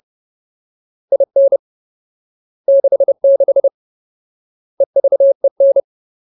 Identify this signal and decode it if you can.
Morse code, heard as a keyed tone, about 30 wpm, 550 Hz. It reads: IN 66 EVEN
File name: Ultimate Word List / 30WPM Ultimate